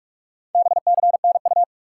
Morse code, heard as a keyed tone, about 44 wpm, 700 Hz. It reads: BCNU
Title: Top 90 QSO Elements 45wpm